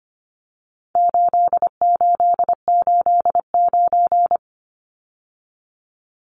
Morse code, heard as a keyed tone, about 25 wpm, 700 Hz. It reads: 8889
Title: Numbers - 4-Digit 25wpm